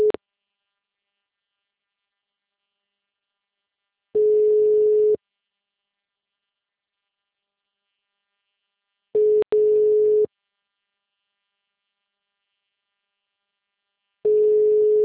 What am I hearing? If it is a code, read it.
Ringback tone (Europe, 425 Hz).